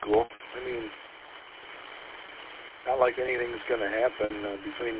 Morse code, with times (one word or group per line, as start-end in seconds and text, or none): none